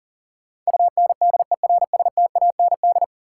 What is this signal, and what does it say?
Morse code, heard as a keyed tone, about 37 wpm, 700 Hz. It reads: UNDERSTAND
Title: Sets of 4 Words - New Word Drill - 40wpm